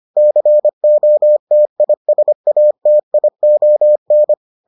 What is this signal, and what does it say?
Morse code, heard as a keyed tone, about 25 wpm, 600 Hz. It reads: COTISATION
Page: bulletin UFT sous forme texte et en CW